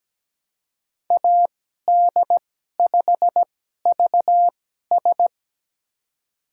Morse code, heard as a keyed tone, about 17 wpm, spaced as for 12 wpm, 700 Hz. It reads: AD5VS
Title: USA Call Signs - 2x Character Spacing - 17wpm